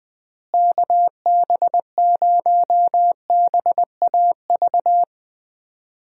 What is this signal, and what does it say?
Morse code, heard as a keyed tone, about 21 wpm, 700 Hz. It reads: KB0BAV
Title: USA Call Signs - 6 Characters - Repeated 2x - 20wpm